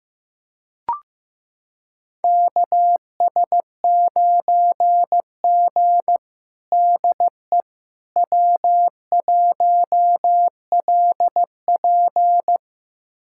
Morse code, heard as a keyed tone, about 15 wpm, 700 Hz. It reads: KS9G DE W1LP
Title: USA Callsign DE Callsign - 4 Characters - 15wpm